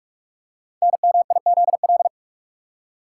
Morse code, heard as a keyed tone, about 44 wpm, 700 Hz. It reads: NMIZL
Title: Code Groups - 5 Letters - 45wpm